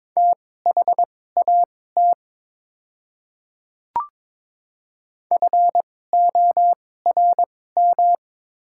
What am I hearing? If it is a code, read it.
Morse code, heard as a keyed tone, about 22 wpm, spaced as for 16 wpm, 700 Hz. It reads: THAT FORM
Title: 2x Character Spacing - Top 200 Words - 22wpm